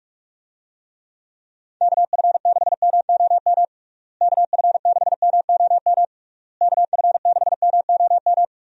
Morse code, heard as a keyed tone, about 41 wpm, 700 Hz. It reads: KF6MOM KF6MOM KF6MOM
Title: USA Call Signs - 6 Characters - Repeated 3x - 45wpm